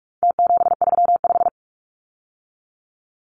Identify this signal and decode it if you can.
Morse code, heard as a keyed tone, about 44 wpm, 700 Hz. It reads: T735
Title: Numbers - Contest Style - 4-Digit 45wpm